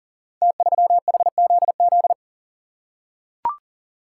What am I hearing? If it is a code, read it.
Morse code, heard as a keyed tone, about 40 wpm, 700 Hz. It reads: T3HZZ
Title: Code Groups - 5 Letter:Numbers - 40wpm